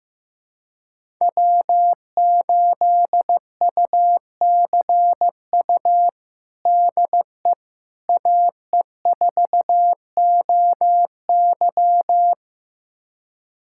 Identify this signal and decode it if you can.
Morse code, heard as a keyed tone, about 15 wpm, 700 Hz. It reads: W8UCU DE AE4OY
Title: USA Callsign DE Callsign 15wpm